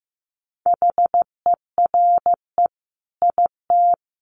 Morse code, heard as a keyed tone, about 15 wpm, 700 Hz. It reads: HERE IT
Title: Sentences from Top 200 Words 15wpm